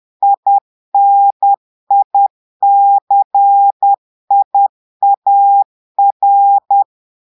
Morse code, heard as a keyed tone, about 10 wpm, 800 Hz. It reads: INICIAR
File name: portugais texte en clair 2 vitesse 10 mots minutes